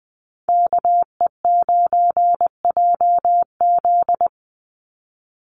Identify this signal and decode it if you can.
Morse code, heard as a keyed tone, about 20 wpm, 700 Hz. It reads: KE9JZ
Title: USA Call Signs 20wpm